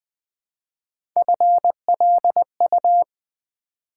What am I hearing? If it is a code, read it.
Morse code, heard as a keyed tone, about 20 wpm, 700 Hz. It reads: FLU
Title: Build A Word - 1 Letter At A Time - 3-Letter Words - 20wpm